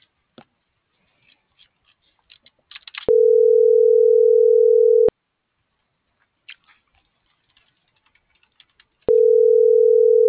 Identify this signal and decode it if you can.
Ringback tone (North America), 440 + 480 Hz.